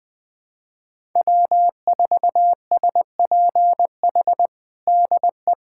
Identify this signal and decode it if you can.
Morse code, heard as a keyed tone, about 20 wpm, 700 Hz. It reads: W4SPH DE
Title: USA Callsign DE Callsign - 5 Characters - 20wpm